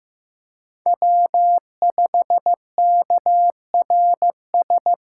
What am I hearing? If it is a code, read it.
Morse code, heard as a keyed tone, about 15 wpm, 700 Hz. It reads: W5KRS
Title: USA Call Signs - 4x Character Spacing - 15wpm